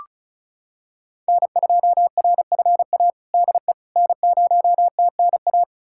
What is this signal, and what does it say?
Morse code, heard as a keyed tone, about 36 wpm, 700 Hz. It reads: N2RFA DE N0TNA
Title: USA Callsign DE Callsign - 5 Characters - 35wpm